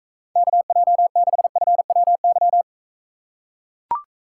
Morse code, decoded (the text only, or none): KJ6FWY